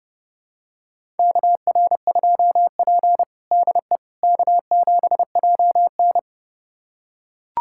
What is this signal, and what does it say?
Morse code, heard as a keyed tone, about 30 wpm, 700 Hz. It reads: KR2P DE K7JN
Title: USA Callsign DE Callsign - 4 Characters - 30wpm